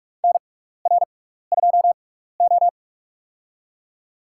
Morse code, heard as a keyed tone, about 46 wpm, spaced as for 15 wpm, 700 Hz. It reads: NR2O
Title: USA Call Signs - 4 Characters - 6x Character Spacing - 45wpm